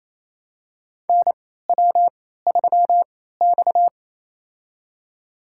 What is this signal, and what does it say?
Morse code, heard as a keyed tone, about 28 wpm, spaced as for 16 wpm, 700 Hz. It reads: NW3X
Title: USA Call Signs - 4 Characters - 3x Character Spacing - 28wpm